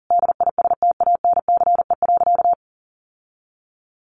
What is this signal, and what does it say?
Morse code, heard as a keyed tone, about 40 wpm, 700 Hz. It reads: DISTANCE.
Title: Sentences from Top 1200 Words 40wpm